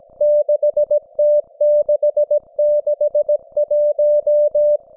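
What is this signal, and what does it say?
Morse code, heard as a keyed tone, about 17 wpm, 590 Hz. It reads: 6T661
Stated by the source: Russian military Morse station, M32, RDL, on 5268 kHz.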